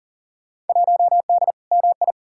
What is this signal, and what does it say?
Morse code, heard as a keyed tone, about 40 wpm, 700 Hz. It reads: 1D MI
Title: Field Day Exchanges - 40wpm